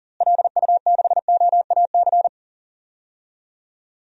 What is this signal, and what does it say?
Morse code, heard as a keyed tone, about 40 wpm, 700 Hz. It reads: LU6OAC
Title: DXCC-100 Call Signs - 6 Characters - Repeated 2x - 40wpm